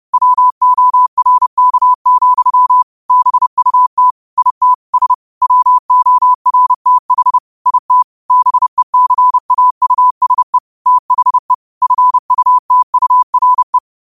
Morse code, heard as a keyed tone, about 30 wpm, 1000 Hz. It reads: WORK, BUT IT S WORTH IT BECAUSE THE FUTURE
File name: qotdfiles / 30wpm